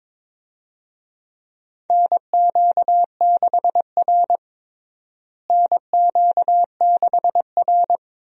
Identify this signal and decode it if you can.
Morse code, heard as a keyed tone, about 22 wpm, 700 Hz. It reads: NQ6R NQ6R
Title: USA Call Signs - 4 Characters - Repeated 2x - 22wpm